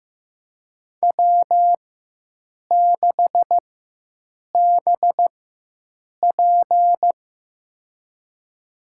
Morse code, heard as a keyed tone, about 15 wpm, spaced as for 7 wpm, 700 Hz. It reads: W6BP
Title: USA Call Signs - 4 Characters - 4x Character Spacing - 15wpm